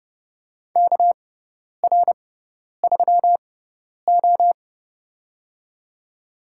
Morse code, heard as a keyed tone, about 30 wpm, spaced as for 10 wpm, 700 Hz. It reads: KR3O
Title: USA Call Signs - 4 Characters - 6x Character Spacing - 30wpm